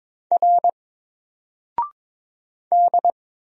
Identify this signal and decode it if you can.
Morse code, heard as a keyed tone, about 23 wpm, 700 Hz. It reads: R D
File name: Single-Letter-Number-22wpm